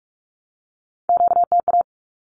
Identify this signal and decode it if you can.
Morse code, heard as a keyed tone, about 44 wpm, 700 Hz. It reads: QTA
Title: Q-Codes 45wpm